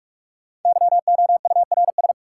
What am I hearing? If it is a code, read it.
Morse code, heard as a keyed tone, about 46 wpm, 700 Hz. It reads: YOURS
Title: Sentences from Top 1500 Words - New Word Drill Alt - 45wpm